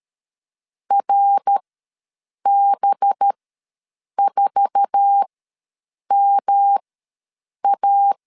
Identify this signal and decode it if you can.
Morse code, heard as a keyed tone, about 13 wpm, spaced as for 7 wpm, 800 Hz. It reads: RB4MA
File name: LEZIONI / 9 2 ws 12wpm 3406